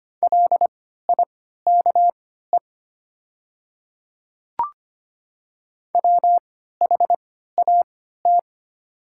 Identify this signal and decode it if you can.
Morse code, heard as a keyed tone, about 25 wpm, spaced as for 14 wpm, 700 Hz. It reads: LIKE WHAT
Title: 3x Character Spacing - Top 200 Words - 25wpm